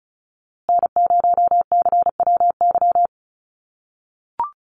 Morse code, heard as a keyed tone, about 34 wpm, 700 Hz. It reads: N0CWY
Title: USA Call Signs Repeated 3x 35wpm